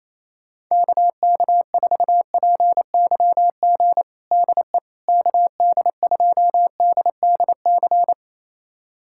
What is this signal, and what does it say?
Morse code, heard as a keyed tone, about 28 wpm, 700 Hz. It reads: KK4PYG DE KD2DDC
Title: USA Callsign DE Callsign 28wpm